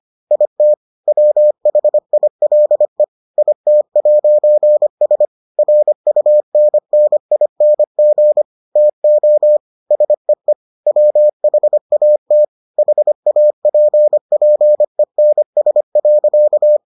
Morse code, heard as a keyed tone, about 25 wpm, 600 Hz. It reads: IT WHILE IT'S RUNNING TO SEE WHAT HAPPENS.